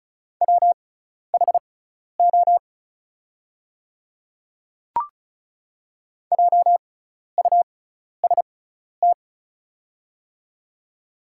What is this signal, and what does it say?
Morse code, heard as a keyed tone, about 36 wpm, spaced as for 12 wpm, 690 Hz. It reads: WHO JUST
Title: 6x Character Spacing - Top 300 Words - 35wpm